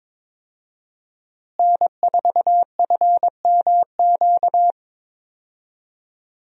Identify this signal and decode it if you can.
Morse code, heard as a keyed tone, about 22 wpm, 700 Hz. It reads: N4FMQ
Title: USA Call Signs - 5 Characters - Repeated 3x - 22wpm